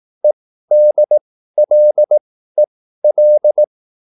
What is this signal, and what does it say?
Morse code, heard as a keyed tone, about 18 wpm, spaced as for 13 wpm, 600 Hz. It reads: EDLEL